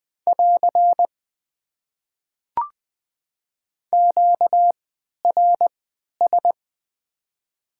Morse code, heard as a keyed tone, about 20 wpm, spaced as for 11 wpm, 700 Hz. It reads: <AR> QRS
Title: Top 90 QSO Elements - 3x Character Spacing - 20wpm